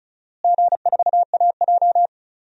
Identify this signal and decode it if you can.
Morse code, heard as a keyed tone, about 36 wpm, 700 Hz. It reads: G4AJ
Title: DXCC-100 Call Signs - 4 Characters - Repeated 2x - 35wpm